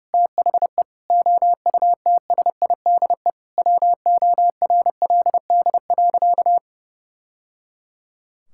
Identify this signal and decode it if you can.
Morse code, heard as a keyed tone, about 30 wpm, 700 Hz. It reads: THE OUTSIDE WORLD.
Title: Sentences from Top 1000 Words - Review Part 6 - 30wpm